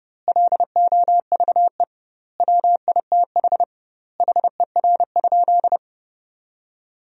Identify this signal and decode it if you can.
Morse code, heard as a keyed tone, about 30 wpm, 700 Hz. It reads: LOVE WITH HER?
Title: 2x Word-Spacing - Sentences from Top 400 Words - 30wpm